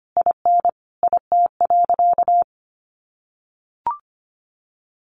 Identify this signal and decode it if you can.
Morse code, heard as a keyed tone, about 25 wpm, 700 Hz. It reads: IN IT.
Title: Sentences from Top 1200 Words 25wpm